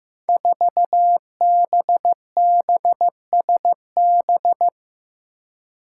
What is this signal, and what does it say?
Morse code, heard as a keyed tone, about 15 wpm, 700 Hz. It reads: 4BBSB
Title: Five Challenging Letter-Number Combinations 15wpm